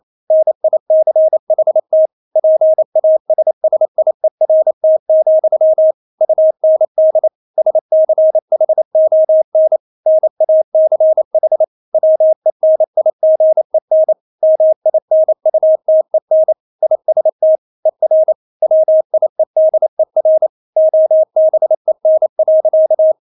Morse code, heard as a keyed tone, about 28 wpm, 630 Hz. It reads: NICHT PASSIERT, UND SCHON NACH WENIGEN MINUTEN IST ER WIEDER OBEN.
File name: Witz9 28wpm